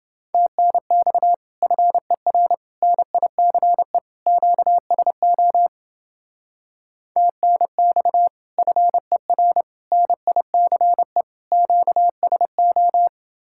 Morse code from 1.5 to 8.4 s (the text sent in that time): FER NICE QSO TNX